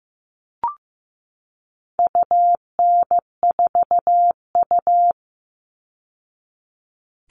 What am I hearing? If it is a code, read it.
Morse code, heard as a keyed tone, about 15 wpm, 700 Hz. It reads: UN4U
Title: All Cut Numbers - 4-Digit 15wpm